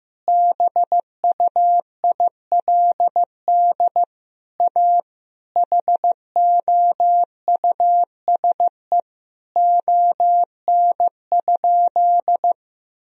Morse code, heard as a keyed tone, about 15 wpm, 700 Hz. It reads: BUILD A HOUSE ON?
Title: Sentences from Top 1000 Words - Review Part 4 - 15wpm